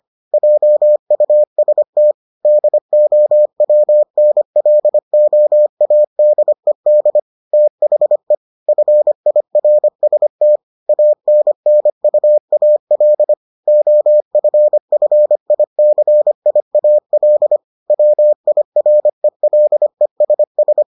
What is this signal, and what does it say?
Morse code, heard as a keyed tone, about 25 wpm, 600 Hz. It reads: JUST DOWNLOADED THE FIRST ANNUAL OFFICIAL WIRELESS